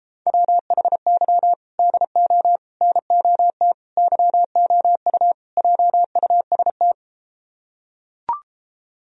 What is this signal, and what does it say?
Morse code, heard as a keyed tone, about 33 wpm, 700 Hz. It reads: WHY DO NOT YOU JUST